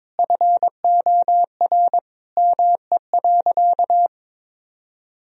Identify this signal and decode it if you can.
Morse code, heard as a keyed tone, about 22 wpm, 700 Hz. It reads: FOR ME.
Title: Sentences from Top 1000 Words - Review Part 4 - 22wpm